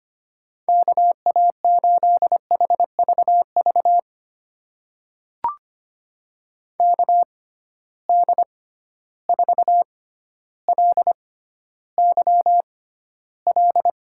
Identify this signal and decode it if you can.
Morse code, heard as a keyed tone, about 25 wpm, spaced as for 13 wpm, 700 Hz. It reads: KA8HVV KD4LYL
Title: USA Call Signs - 6 Characters - 6x Character Spacing - 25wpm